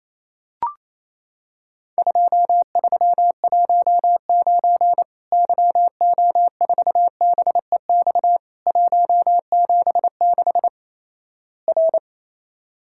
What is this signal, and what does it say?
Morse code, heard as a keyed tone, about 28 wpm, 700 Hz. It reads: 2319 YO4BEX 176 R
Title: Worked All Europe DX Contest - QTC Exchange - 28wpm